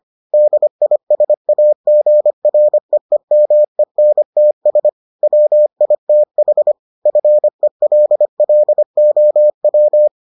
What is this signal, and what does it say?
Morse code, heard as a keyed tone, about 25 wpm, 600 Hz. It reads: DISAGREEMENTS WITH FELLOW